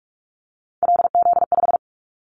Morse code, heard as a keyed tone, about 46 wpm, 700 Hz. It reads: LZ5
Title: Callsign Prefixes for Top 25 DXCC - 45wpm